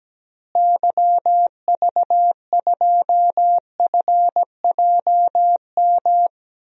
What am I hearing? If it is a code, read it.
Morse code, heard as a keyed tone, about 17 wpm, 700 Hz. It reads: YV2FJM
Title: DXCC-100 Call Signs - 6 Characters - Repeated 2x - 17wpm